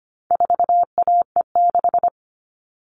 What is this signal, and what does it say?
Morse code, heard as a keyed tone, about 25 wpm, 700 Hz. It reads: 4AE6
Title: All Cut Numbers - 4-Digit 25wpm